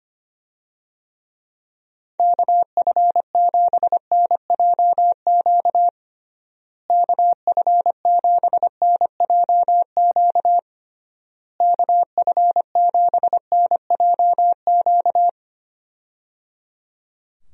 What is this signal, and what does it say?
Morse code, heard as a keyed tone, about 25 wpm, 700 Hz. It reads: KF7NJQ KF7NJQ KF7NJQ
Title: USA Call Signs - 6 Characters - Repeated 3x - 25wpm